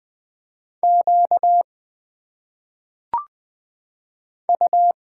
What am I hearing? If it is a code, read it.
Morse code, heard as a keyed tone, about 20 wpm, 700 Hz. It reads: Q U
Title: Single Letters - Phonetic Pronunciation 20wpm